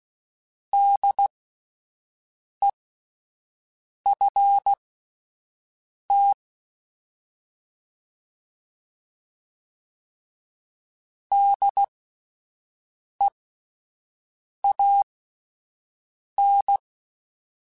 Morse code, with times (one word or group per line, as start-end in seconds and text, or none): 0.7-6.3 DEFT
11.3-16.8 DEAN